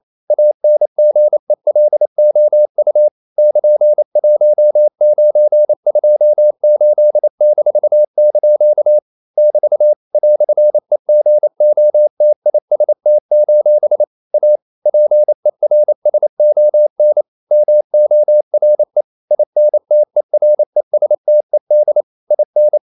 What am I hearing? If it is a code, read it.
Morse code, heard as a keyed tone, about 28 wpm, 600 Hz. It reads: ANGELOU <KN>1928-) <BT> "EGOTIST: A PERSON MORE INTERESTED IN
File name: CW / Practice / EnglishQuotations0228